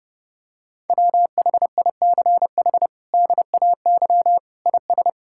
Morse code, heard as a keyed tone, about 32 wpm, 700 Hz. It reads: WHICH DAY IS
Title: Sentences from Top 1000 Words - Review Part 5 - 30wpm